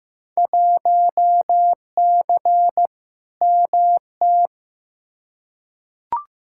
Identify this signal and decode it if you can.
Morse code, heard as a keyed tone, about 15 wpm, 700 Hz. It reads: 1C MT
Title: Field Day Exchanges Encore - 15wpm